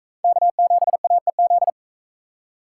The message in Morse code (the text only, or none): K7AEZ